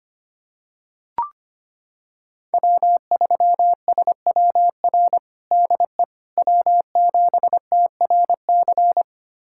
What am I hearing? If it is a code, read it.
Morse code, heard as a keyed tone, about 25 wpm, 700 Hz. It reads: W3SWR DE W7TRC
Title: USA Callsign DE Callsign - 5 Characters - 25wpm